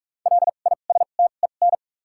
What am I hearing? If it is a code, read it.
Morse code, heard as a keyed tone, about 44 wpm, spaced as for 32 wpm, 700 Hz. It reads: LISTEN
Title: 2x Character Spacing - Top 500 Words - 45wpm